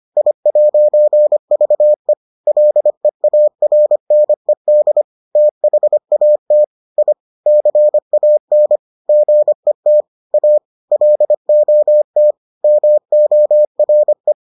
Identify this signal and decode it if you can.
Morse code, heard as a keyed tone, about 25 wpm, 600 Hz. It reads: I'VE LEARNED THAT I CAN GET A LOT MORE